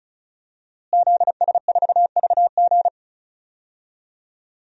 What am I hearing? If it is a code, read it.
Morse code, heard as a keyed tone, about 34 wpm, 700 Hz. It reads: ZS4VG
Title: DXCC-100 Call Signs - 5 Characters - Repeated 3x - 35wpm